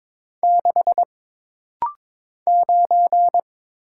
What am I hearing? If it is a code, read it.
Morse code, heard as a keyed tone, about 22 wpm, 700 Hz. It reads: 6 9
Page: This rapid-fire timing is geared to push your ICR (Instant Character Recognition) skills.